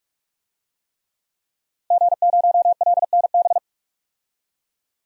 Morse code, heard as a keyed tone, about 46 wpm, 700 Hz. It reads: G0LNB
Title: DXCC-100 Call Signs - 5 Characters - Repeated 3x - 45wpm